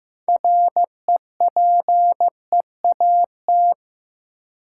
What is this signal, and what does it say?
Morse code, heard as a keyed tone, about 15 wpm, 700 Hz. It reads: REPEAT
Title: Sentences from Top 1600 Words - New Word Drill - 15wpm